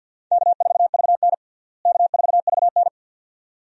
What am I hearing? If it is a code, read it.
Morse code, heard as a keyed tone, about 53 wpm, 700 Hz. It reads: K4VN K4VN
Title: USA Call Signs - 4 Characters - Repeated 2x - 50wpm